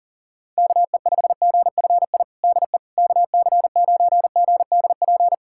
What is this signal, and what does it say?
Morse code, heard as a keyed tone, about 40 wpm, 700 Hz. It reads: KE5GFI DE KC9GDP
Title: USA Callsign DE Callsign 40wpm